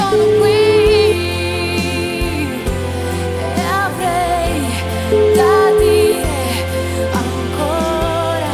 Ringback tone (Europe, 425 Hz), with noise.